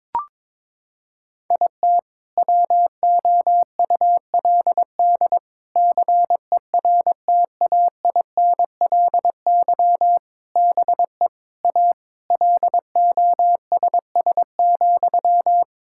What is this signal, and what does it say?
Morse code, heard as a keyed tone, about 22 wpm, 700 Hz. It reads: IT WOULD CERTAINLY BE A LOSS,
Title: Sentences from Top 1600 Words 22wpm